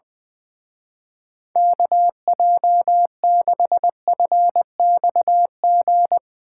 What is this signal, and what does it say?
Morse code, heard as a keyed tone, about 22 wpm, 700 Hz. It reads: KJ6FXG
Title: Speed Racing - USA 2x3 Call Signs 20wpm